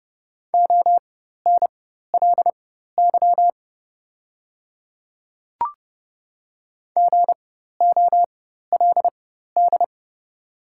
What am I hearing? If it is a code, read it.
Morse code, heard as a keyed tone, about 30 wpm, spaced as for 14 wpm, 700 Hz. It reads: ONLY GOLD
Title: 4x Character Spacing - Top 500 Words - 30wpm